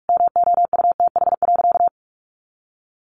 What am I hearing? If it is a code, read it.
Morse code, heard as a keyed tone, about 44 wpm, 700 Hz. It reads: MOUTH.
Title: Sentences from Top 1500 Words 45wpm